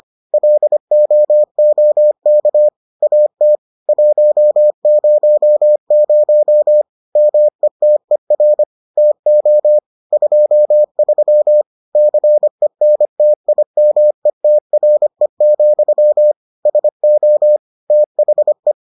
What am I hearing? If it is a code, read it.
Morse code, heard as a keyed tone, about 25 wpm, 600 Hz. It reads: LOOK AT 100 METER TO 23 CENTIMETRE, SO THE